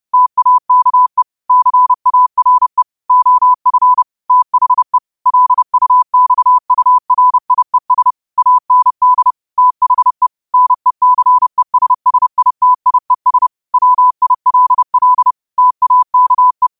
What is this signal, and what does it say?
Morse code, heard as a keyed tone, about 30 wpm, 1000 Hz. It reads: TAKE CARE OF THE LUXURIES AND THE NECESSITIES WILL TAKE
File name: qotdfiles / 30wpm